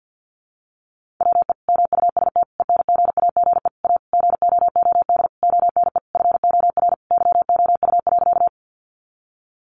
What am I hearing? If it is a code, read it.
Morse code, heard as a keyed tone, about 50 wpm, 700 Hz. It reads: WE MUST ENGAGE A GOOD ONE FOR YOU.
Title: Sentences from Top 1500 Words 50wpm